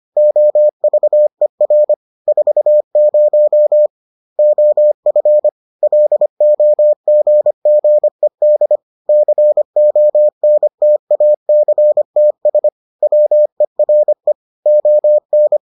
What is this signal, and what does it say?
Morse code, heard as a keyed tone, about 25 wpm, 600 Hz. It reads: OVER 40 OF LOGGED CONTACTS WERE ON